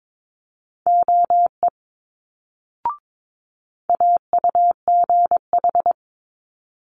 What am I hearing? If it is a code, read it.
Morse code, heard as a keyed tone, about 22 wpm, 700 Hz. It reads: OE AUGH
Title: Multi-letter Phonemes Quick Fire 22wpm